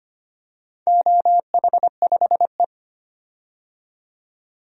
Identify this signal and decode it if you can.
Morse code, heard as a keyed tone, about 25 wpm, 700 Hz. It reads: OH5E
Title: DXCC-100 Call Signs - 4 Characters - 25wpm